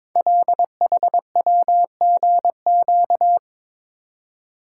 Morse code, heard as a keyed tone, about 22 wpm, 700 Hz. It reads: LHWGQ
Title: Code Groups - 5 Letters - 22wpm